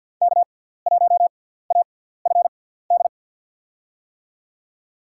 Morse code, heard as a keyed tone, about 46 wpm, spaced as for 17 wpm, 700 Hz. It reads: K1AFD
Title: USA Call Signs - 6x Character Spacing - 50wpm